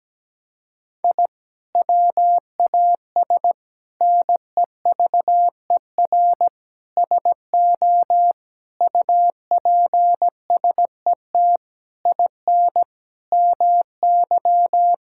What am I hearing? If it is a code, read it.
Morse code, heard as a keyed tone, about 17 wpm, 700 Hz. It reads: I WAS NEVER SO UPSET IN MY
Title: Sentences from Top 1000 Words - Review Part 5 - 17wpm